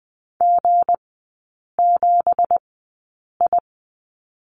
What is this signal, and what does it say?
Morse code, heard as a keyed tone, about 20 wpm, spaced as for 8 wpm, 700 Hz. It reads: G7I
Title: 166 Lesson 27e - Number 7 - Three Characters 20wpm